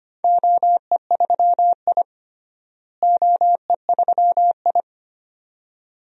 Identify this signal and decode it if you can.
Morse code, heard as a keyed tone, about 25 wpm, 700 Hz. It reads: OE3I OE3I
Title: DXCC-100 Call Signs - 4 Characters - Repeated 2x - 25wpm